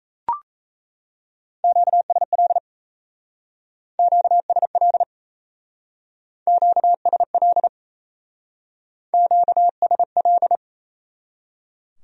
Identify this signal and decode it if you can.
Morse code, heard as a keyed tone, about 36 wpm, 700 Hz. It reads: QSL QSL QSL QSL